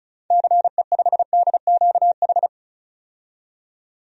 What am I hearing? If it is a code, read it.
Morse code, heard as a keyed tone, about 34 wpm, 700 Hz. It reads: CE5DQH
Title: DXCC-100 Call Signs - 6 Characters - 35wpm